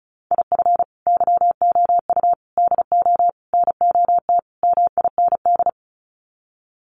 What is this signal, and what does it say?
Morse code, heard as a keyed tone, about 34 wpm, 700 Hz. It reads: IF YOU DO NOT MIND
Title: Sets of 5 Words 35wpm